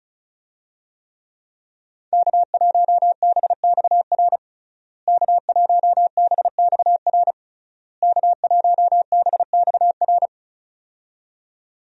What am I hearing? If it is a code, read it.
Morse code, heard as a keyed tone, about 34 wpm, 700 Hz. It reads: K1BXR K1BXR K1BXR
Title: USA Call Signs - 5 Characters - Repeated 3x - 35wpm